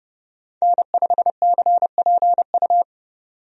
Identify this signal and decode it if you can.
Morse code, heard as a keyed tone, about 30 wpm, 700 Hz. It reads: N5CPU
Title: USA Call Signs - 5 Characters - 30wpm